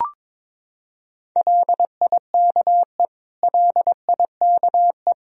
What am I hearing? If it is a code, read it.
Morse code, heard as a keyed tone, about 22 wpm, 700 Hz. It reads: LIKE LIKE
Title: Top 100 Words Repeated 22wpm